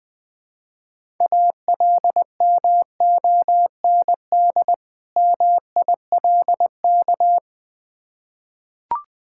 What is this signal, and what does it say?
Morse code, heard as a keyed tone, about 21 wpm, 700 Hz. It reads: ALMOND MILK
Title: Difficult Words with Hints - 20wpm